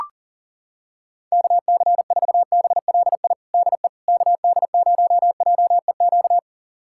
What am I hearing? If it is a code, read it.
Morse code, heard as a keyed tone, about 40 wpm, 700 Hz. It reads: KC4BLI DE KD0JEQ
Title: USA Callsign DE Callsign 40wpm